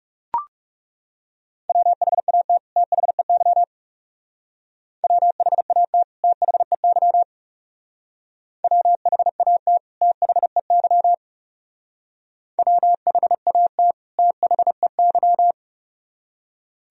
Morse code, heard as a keyed tone, about 37 wpm, 700 Hz. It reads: WHAT THEY WHAT THEY WHAT THEY WHAT THEY
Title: Speed-Racing - Sets of 2 Words 30wpm